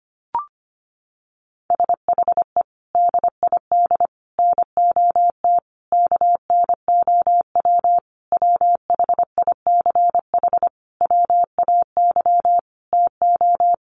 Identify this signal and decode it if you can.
Morse code, heard as a keyed tone, about 25 wpm, 700 Hz. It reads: SHE DID NOT KNOW WHICH WAY TO
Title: Sentences from Top 1100 Words 25wpm